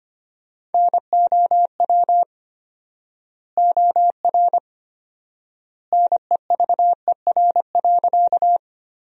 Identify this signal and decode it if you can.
Morse code, heard as a keyed tone, about 25 wpm, 700 Hz. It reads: NOW OR NEVER.
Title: 4x Word-Spacing - Sentences from Top 300 Words - 25wpm